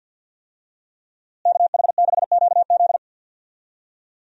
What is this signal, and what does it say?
Morse code, heard as a keyed tone, about 50 wpm, 700 Hz. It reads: KH6QZ
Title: USA Call Signs - 5 Characters - Repeated 2x - 50wpm